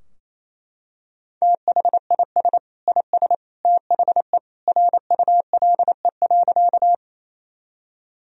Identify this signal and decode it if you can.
Morse code, heard as a keyed tone, about 28 wpm, 700 Hz. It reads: THIS IS THE RULE.
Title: Sentences from Top 1000 Words - Review Part 2 - 28wpm